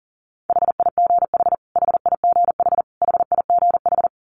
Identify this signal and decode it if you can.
Morse code, heard as a keyed tone, about 40 wpm, 700 Hz. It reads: HIGH HIGH HIGH
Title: Top 500 Words Repeated 40wpm